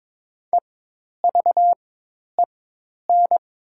Morse code, heard as a keyed tone, about 22 wpm, spaced as for 10 wpm, 700 Hz. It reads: EVEN
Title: 4x Character Spacing - Top 200 Words - 22wpm